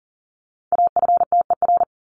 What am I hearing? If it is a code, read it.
Morse code, heard as a keyed tone, about 40 wpm, 700 Hz. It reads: AFTER
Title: Top 300 Words Repeated 40wpm